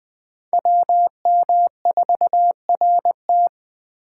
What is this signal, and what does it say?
Morse code, heard as a keyed tone, about 20 wpm, 700 Hz. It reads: WM4RT
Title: Speed Racing - USA 2x2 Call Signs 17wpm